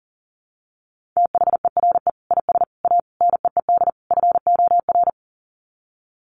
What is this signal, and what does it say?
Morse code, heard as a keyed tone, about 40 wpm, 700 Hz. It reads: THERE IS A NEED FOR
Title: Sets of 5 Words Encore2 40wpm